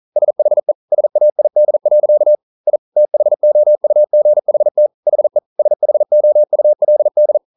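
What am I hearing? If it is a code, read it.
Morse code, heard as a keyed tone, about 40 wpm, 600 Hz. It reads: SHE SAID. I THOUGHT HE SHOULD
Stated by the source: Morse code practice at 41 wpm, from CW med Gnister.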